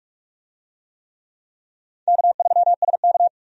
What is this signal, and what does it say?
Morse code, heard as a keyed tone, about 46 wpm, 700 Hz. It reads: K3SK
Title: USA Call Signs - 4 Characters - 45wpm